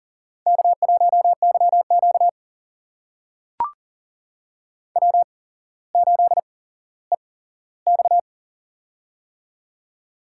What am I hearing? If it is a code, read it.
Morse code, heard as a keyed tone, about 40 wpm, spaced as for 17 wpm, 700 Hz. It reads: K1YQ W8EX
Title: USA Call Signs - 4 Characters - 8x Character Spacing - 40wpm